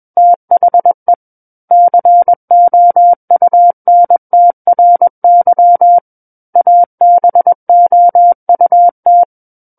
Morse code, heard as a keyed tone, about 21 wpm, 700 Hz. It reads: THE COUNTRY ABOUT
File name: Book One / 12-Chapter 12 - 21 WPM